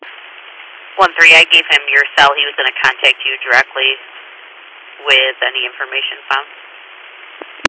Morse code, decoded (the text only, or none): none